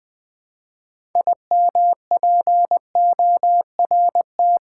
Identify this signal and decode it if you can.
Morse code, heard as a keyed tone, about 20 wpm, 700 Hz. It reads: IMPORT
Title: 6 Letter Words - First 3 Letters Given - 20wpm